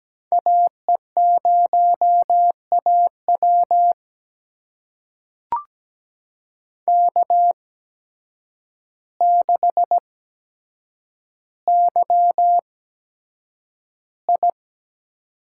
Morse code, heard as a keyed tone, about 17 wpm, spaced as for 8 wpm, 700 Hz. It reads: AE0AW K6YI
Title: USA Call Signs - 8x Character Spacing - 17wpm